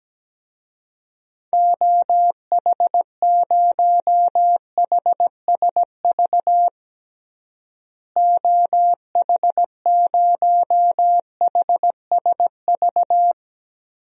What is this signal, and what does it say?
Morse code, heard as a keyed tone, about 17 wpm, 700 Hz. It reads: OH0HSV OH0HSV
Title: DXCC-100 Call Signs - 6 Characters - Repeated 2x - 17wpm